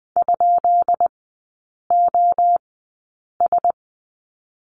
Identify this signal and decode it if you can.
Morse code, heard as a keyed tone, about 20 wpm, spaced as for 8 wpm, 690 Hz. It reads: ?OS
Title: 145 Lesson 24e - Letter y - Three Characters 20wpm